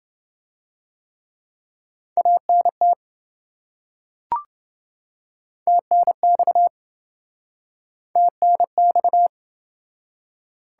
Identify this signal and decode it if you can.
Morse code, heard as a keyed tone, about 26 wpm, 700 Hz. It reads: ANT TNX TNX